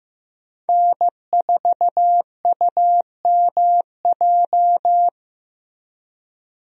Morse code, heard as a keyed tone, about 16 wpm, 700 Hz. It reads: N4UMJ
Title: USA Call Signs - 5 Characters - Repeated 2x - 15wpm